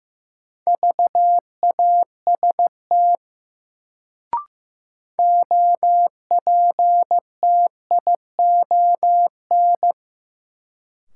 Morse code, heard as a keyed tone, about 15 wpm, 700 Hz. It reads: VAST OPTION
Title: Sentences from Top 1300 Words - New Word Drill - 15wpm